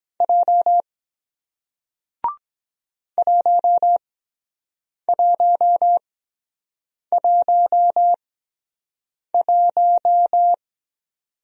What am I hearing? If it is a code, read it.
Morse code, heard as a keyed tone, about 24 wpm, spaced as for 7 wpm, 700 Hz. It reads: J 1111